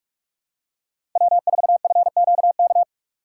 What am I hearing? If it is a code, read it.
Morse code, heard as a keyed tone, about 47 wpm, 700 Hz. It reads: W4FQK
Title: USA Call Signs - 5 Characters - 6x Character Spacing - 45wpm